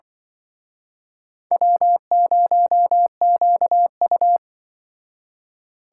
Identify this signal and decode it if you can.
Morse code, heard as a keyed tone, about 24 wpm, 700 Hz. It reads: W0QU
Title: Speed Racing - USA 1x2 Call Signs 20wpm